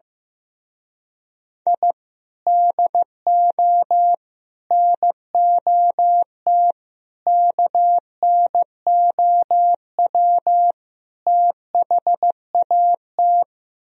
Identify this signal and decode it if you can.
Morse code, heard as a keyed tone, about 15 wpm, 700 Hz. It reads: I DO NOT KNOW THAT
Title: Speed-Racing - Sets of 5 Words 15wpm